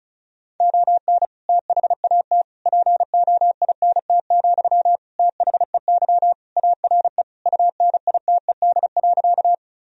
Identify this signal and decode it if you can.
Morse code, heard as a keyed tone, about 36 wpm, 700 Hz. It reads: ON THAT POINT, THEY ARE UNITED.
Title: Sentences from Top 1000 Words - Review Part 4 - 35wpm